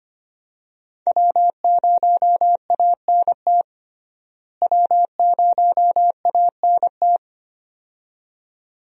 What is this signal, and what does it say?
Morse code, heard as a keyed tone, about 25 wpm, 700 Hz. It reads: W0ANT W0ANT
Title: USA Call Signs - 5 Characters - Repeated 2x - 25wpm